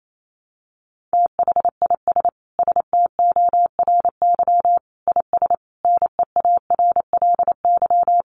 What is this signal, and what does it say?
Morse code, heard as a keyed tone, about 28 wpm, 700 Hz. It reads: THIS STORY IS NEARLY
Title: Sentences from Top 1300 Words 28wpm